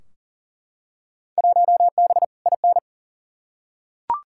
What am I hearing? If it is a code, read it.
Morse code, heard as a keyed tone, about 40 wpm, 700 Hz. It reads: 1B IN
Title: Field Day Exchanges Encore - 40wpm